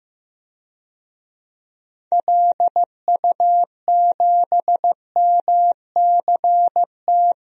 Morse code, heard as a keyed tone, about 16 wpm, 700 Hz. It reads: LU7MCT
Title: DXCC-100 Call Signs - 6 Characters - 15wpm